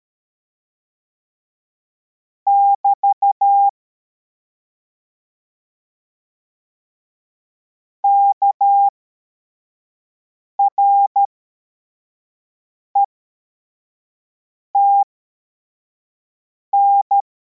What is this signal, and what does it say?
Morse code, heard as a keyed tone, about 13 wpm, spaced as for 4 wpm, 800 Hz. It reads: <BT> KRETN
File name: formation / cw / Text41